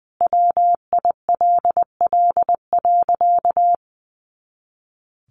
Morse code, heard as a keyed tone, about 20 wpm, 700 Hz. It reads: WILL.
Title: Sentences from Top 100 Words 20wpm